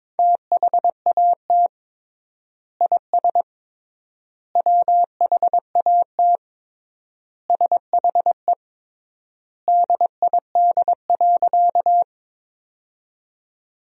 Morse code, heard as a keyed tone, about 22 wpm, 700 Hz. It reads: THAT IS WHAT SHE DID.
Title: 3x Word-Spacing - Sentences from Top 100 Words - 22wpm